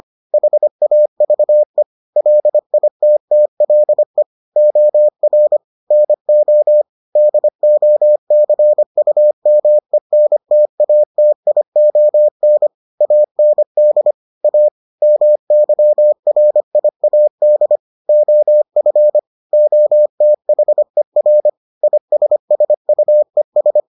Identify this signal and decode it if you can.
Morse code, heard as a keyed tone, about 25 wpm, 600 Hz. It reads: HAVE LITTLE OR NO DOCUMENTATION AND A MYRIAD OF OTHER ISSUES